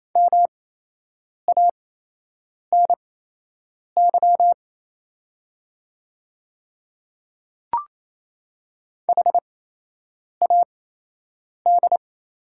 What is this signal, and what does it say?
Morse code, heard as a keyed tone, about 28 wpm, spaced as for 8 wpm, 690 Hz. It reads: MANY HAD
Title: 8x Character Spacing - Top 100 Words - 28wpm